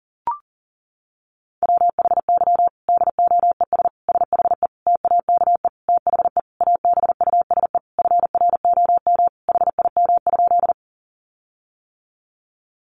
Morse code, heard as a keyed tone, about 40 wpm, 700 Hz. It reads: WHY DOES SHE TAKE THE ABUSE FROM HIM?
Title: Sentences from Top 1300 Words 40wpm